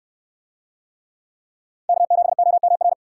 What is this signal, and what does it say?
Morse code, heard as a keyed tone, about 68 wpm, 700 Hz. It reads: K7OMR